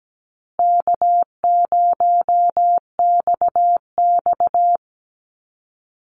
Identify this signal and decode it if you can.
Morse code, heard as a keyed tone, about 17 wpm, 700 Hz. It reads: K0XX
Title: Easy Call Signs Repeated 3x - 17wpm